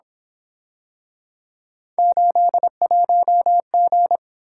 Morse code, heard as a keyed tone, about 26 wpm, 700 Hz. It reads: 81G
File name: Speed-Racing - Three Letter-Number Combinations - 22wpm